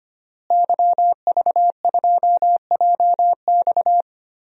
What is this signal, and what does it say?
Morse code, heard as a keyed tone, about 25 wpm, 700 Hz. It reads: YV2JX
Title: DXCC-100 Call Signs - 5 Characters - Repeated 3x - 25wpm